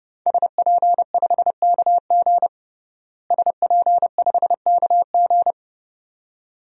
Morse code, heard as a keyed tone, about 30 wpm, 700 Hz. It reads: SP5KG SP5KG
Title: DXCC-100 Call Signs - 5 Characters - Repeated 2x - 30wpm